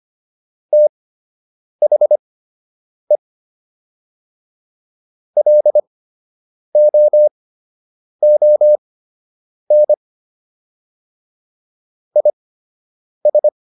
Morse code, heard as a keyed tone, about 25 wpm, spaced as for 8 wpm, 600 Hz. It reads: THE LOON IS